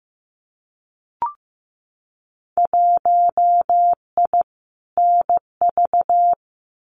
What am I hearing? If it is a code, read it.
Morse code, heard as a keyed tone, about 15 wpm, 700 Hz. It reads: 1I NV
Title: Winter Field Day Exchanges Encore 15wpm